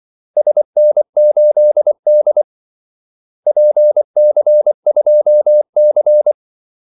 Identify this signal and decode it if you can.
Morse code, heard as a keyed tone, about 24 wpm, 600 Hz. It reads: SN8D PC2C
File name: Callsigns-24wpm